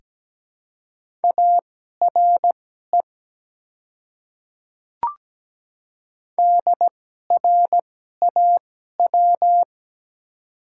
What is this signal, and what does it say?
Morse code, heard as a keyed tone, about 17 wpm, spaced as for 12 wpm, 700 Hz. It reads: ARE DRAW
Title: 2x Character Spacing - Top 300 Words - 17wpm